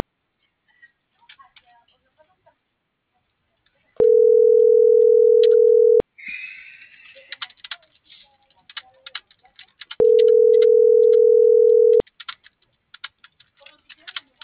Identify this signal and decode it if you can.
Ringback tone (North America), 440 + 480 Hz.
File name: CENTRAL SAGOT